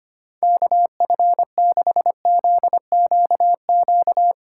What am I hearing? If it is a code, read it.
Morse code, heard as a keyed tone, about 32 wpm, 700 Hz. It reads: KF6ZQQ